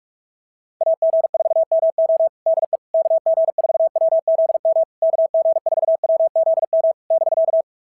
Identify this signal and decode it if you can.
Morse code, heard as a keyed tone, about 44 wpm, 690 Hz. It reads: AG4MO DE KG4WZM KG4WZM <BK>
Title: SOTA QSO Practice 45wpm